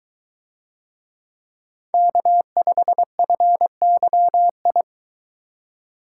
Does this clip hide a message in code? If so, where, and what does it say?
Yes: Morse code from 1.9 to 4.8 s, heard as a keyed tone, about 22 wpm, 700 Hz. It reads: K5FYI